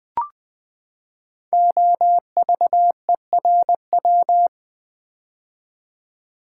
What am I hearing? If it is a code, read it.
Morse code, heard as a keyed tone, about 20 wpm, 700 Hz. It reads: OVERW
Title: Build A Word - 5 Letters At A Time - 12-Letter Words - 20wpm